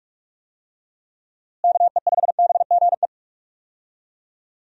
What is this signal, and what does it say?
Morse code, heard as a keyed tone, about 46 wpm, 700 Hz. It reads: KE5BGE
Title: USA Call Signs - 6 Characters - Repeated 2x - 45wpm